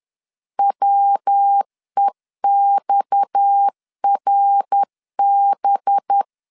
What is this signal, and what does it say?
Morse code, heard as a keyed tone, about 10 wpm, 800 Hz. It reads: WEXRB
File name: CW / DET / 1 6 1500C S B 10wpm 4137